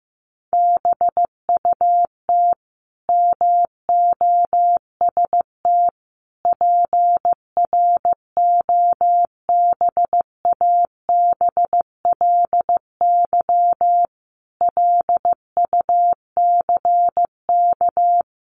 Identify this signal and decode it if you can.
Morse code, heard as a keyed tone, about 15 wpm, 700 Hz. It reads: BUT MOST PROBABLY LUCK
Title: Sentences from Top 1500 Words 15wpm